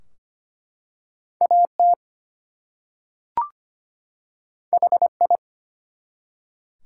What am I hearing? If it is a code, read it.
Morse code, heard as a keyed tone, about 25 wpm, 700 Hz. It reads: AT HI
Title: Two Letter Words - 25wpm